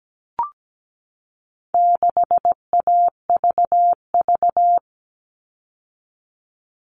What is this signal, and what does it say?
Morse code, heard as a keyed tone, about 17 wpm, 700 Hz. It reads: 6AVV
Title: All Cut Numbers - 4-Digit 17wpm